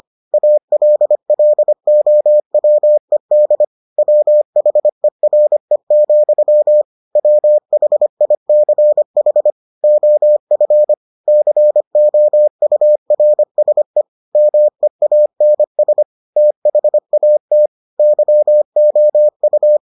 Morse code, heard as a keyed tone, about 25 wpm, 600 Hz. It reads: ALLOWED WHERE, WHICH OF COURSE MEANS THAT YOU